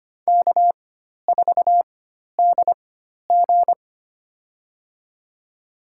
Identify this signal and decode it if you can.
Morse code, heard as a keyed tone, about 25 wpm, spaced as for 12 wpm, 700 Hz. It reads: K4DG
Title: USA Call Signs - 4x Character Spacing - 25wpm